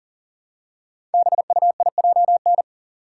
Morse code, heard as a keyed tone, about 38 wpm, 700 Hz. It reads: BUIJN
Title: Code Groups - 5 Letters - 40wpm